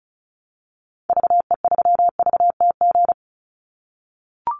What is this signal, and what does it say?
Morse code, heard as a keyed tone, about 34 wpm, 700 Hz. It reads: VE3VTG
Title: Call Signs - 35wpm